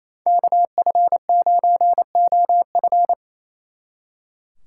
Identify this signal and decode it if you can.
Morse code, heard as a keyed tone, about 33 wpm, 700 Hz. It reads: KF9OF
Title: Speed Racing - USA 2x2 Call Signs 28wpm